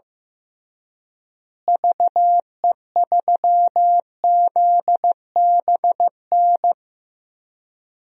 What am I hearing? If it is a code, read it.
Morse code, heard as a keyed tone, about 15 wpm, 700 Hz. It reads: VE3ZBN
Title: DXCC-100 Call Signs - 6 Characters - Repeated 2x - 15wpm